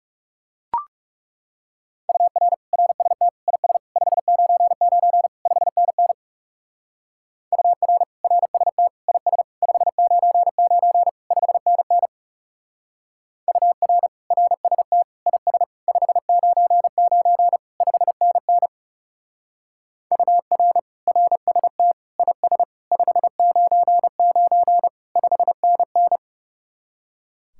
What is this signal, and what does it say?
Morse code, heard as a keyed tone, about 39 wpm, 700 Hz. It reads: UR RST IS 599 5NN UR RST IS 599 5NN UR RST IS 599 5NN UR RST IS 599 5NN